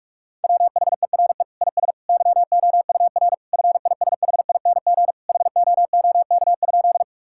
Morse code, heard as a keyed tone, about 46 wpm, 700 Hz. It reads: WHERE IS YOUR FISHING HOOK?
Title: Sentences from Top 1600 Words 45wpm